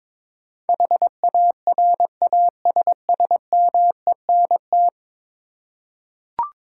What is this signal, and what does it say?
Morse code, heard as a keyed tone, about 22 wpm, 700 Hz. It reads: HARASSMENT
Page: morse-code-ninja-practice-22wpm